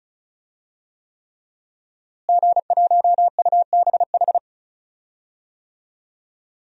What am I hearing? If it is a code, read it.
Morse code, heard as a keyed tone, about 36 wpm, 700 Hz. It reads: G1UBH
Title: DXCC-100 Call Signs - 5 Characters - Repeated 3x - 35wpm